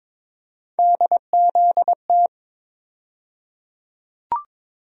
Morse code, heard as a keyed tone, about 22 wpm, 700 Hz. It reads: DZT
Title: USA Call Sign Postfixes - 3 Letters 22wpm